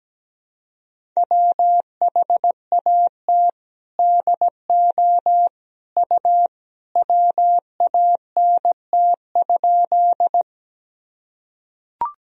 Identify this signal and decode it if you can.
Morse code, heard as a keyed tone, about 17 wpm, 700 Hz. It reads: WHAT DO U WANT?
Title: Sentences from Top 200 Words and Abbreviations 17wpm